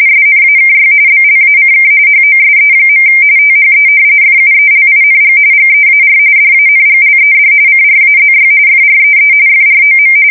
RTTY, about 45.45 baud. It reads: SNAKY BLACK HOSE. THE FUELING GEAR DANGLED CRAZY FREE FROM
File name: fsk170 / battleshipnj